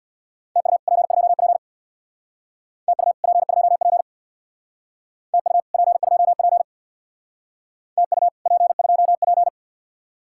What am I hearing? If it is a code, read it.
Morse code, heard as a keyed tone, about 59 wpm, 700 Hz. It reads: TU P2P TU P2P TU P2P TU P2P
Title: Speed-Racing Common QSO Phrases Encore - 50wpm